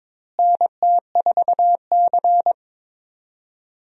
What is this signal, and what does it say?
Morse code, heard as a keyed tone, about 22 wpm, 700 Hz. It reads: NT4C
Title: USA Call Signs - 4 Characters - Repeated 2x - 22wpm